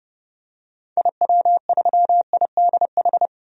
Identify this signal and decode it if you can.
Morse code, heard as a keyed tone, about 30 wpm, 700 Hz. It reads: IW3IDH
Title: DXCC-100 Call Signs - 6 Characters - 30wpm